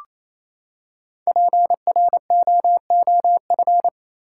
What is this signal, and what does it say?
Morse code, heard as a keyed tone, about 28 wpm, 700 Hz. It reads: PROOF
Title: Five Letter Words - 28wpm